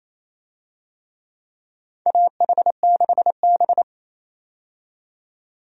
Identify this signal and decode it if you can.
Morse code, heard as a keyed tone, about 28 wpm, 700 Hz. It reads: AH6B
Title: USA Call Signs - 4 Characters - 28wpm